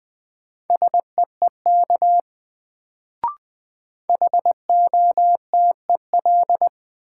Morse code, heard as a keyed tone, about 21 wpm, 700 Hz. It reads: SEEK HOTEL
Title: Sentences from Top 1600 Words - New Word Drill - 20wpm